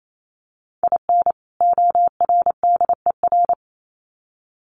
Morse code, heard as a keyed tone, about 28 wpm, 700 Hz. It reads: IN ORDER
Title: Sets of 2 Words 28wpm